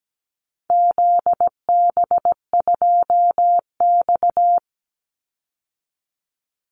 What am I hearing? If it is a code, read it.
Morse code, heard as a keyed tone, about 17 wpm, 700 Hz. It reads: ZB2X
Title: Call Signs Repeated 2x - 17wpm